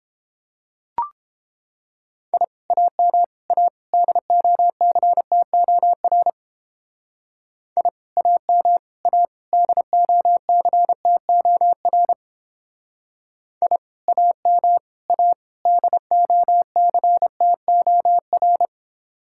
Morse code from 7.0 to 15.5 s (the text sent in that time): I AM A DOCTOR I AM A